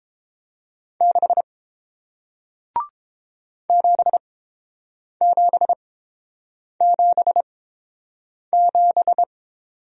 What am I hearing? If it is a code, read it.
Morse code, heard as a keyed tone, about 32 wpm, spaced as for 7 wpm, 700 Hz. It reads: 6 7777